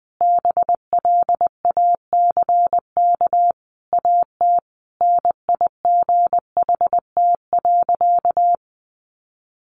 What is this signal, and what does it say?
Morse code, heard as a keyed tone, about 19 wpm, 700 Hz. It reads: BLACK AT NIGHT.
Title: Sentences from Top 1300 Words 20wpm